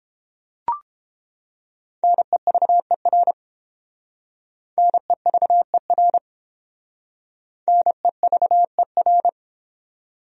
Morse code, heard as a keyed tone, about 29 wpm, 700 Hz. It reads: NEVER NEVER NEVER